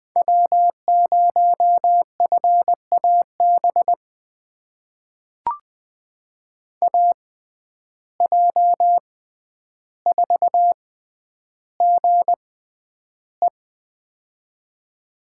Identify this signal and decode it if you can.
Morse code, heard as a keyed tone, about 20 wpm, spaced as for 10 wpm, 700 Hz. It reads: W0FAB AJ4GE
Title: USA Call Signs - 5 Characters - 6x Character Spacing - 20wpm